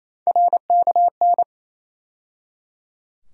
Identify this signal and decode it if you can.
Morse code, heard as a keyed tone, about 28 wpm, 700 Hz. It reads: RKN
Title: USA Call Sign Postfixes - 3 Letters 28wpm